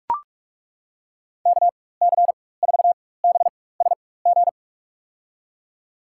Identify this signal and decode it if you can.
Morse code, heard as a keyed tone, about 41 wpm, spaced as for 20 wpm, 700 Hz. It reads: KC4BSG
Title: USA Call Signs - 6 Characters - 4x Character Spacing - 45wpm